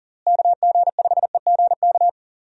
Morse code, heard as a keyed tone, about 37 wpm, 700 Hz. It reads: KG5EGK
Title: USA Call Signs - 6 Characters - Repeated 2x - 40wpm